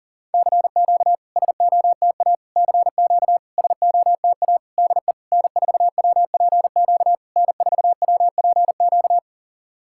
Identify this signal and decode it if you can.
Morse code, heard as a keyed tone, about 40 wpm, 690 Hz. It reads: CQ SOTA CQ SOTA DE N4WPQ N4WPQ
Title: SOTA QSO Practice 40wpm